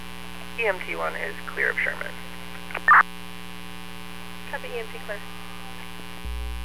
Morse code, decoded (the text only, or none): none